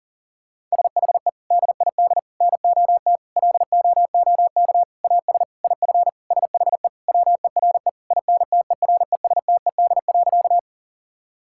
Morse code, heard as a keyed tone, about 37 wpm, 700 Hz. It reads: SHE DID NOT LOOK AS IF SHE WERE INTERESTED.
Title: Sentences from Top 1000 Words - Review Part 6 - 40wpm